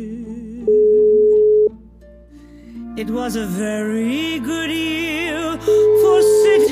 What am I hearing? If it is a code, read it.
Ringback tone (Europe, 425 Hz), with noise.